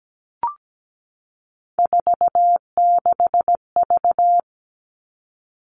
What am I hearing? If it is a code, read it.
Morse code, heard as a keyed tone, about 17 wpm, 700 Hz. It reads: 46V
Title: All Cut Numbers - 3-Digit 17wpm